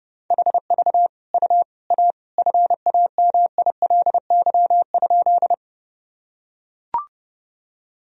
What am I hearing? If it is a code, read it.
Morse code, heard as a keyed tone, about 30 wpm, 700 Hz. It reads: HV U A FAMILY?
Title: Sentences from Top 400 Words and Abbreviations 30wpm